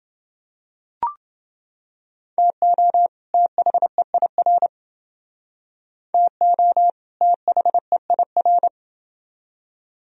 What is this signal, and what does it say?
Morse code, heard as a keyed tone, about 28 wpm, 700 Hz. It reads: TO THEIR TO THEIR